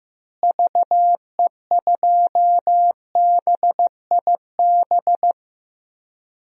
Morse code, heard as a keyed tone, about 15 wpm, 700 Hz. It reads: VE2BIB
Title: DXCC-100 Call Signs - 6 Characters - Repeated 2x - 15wpm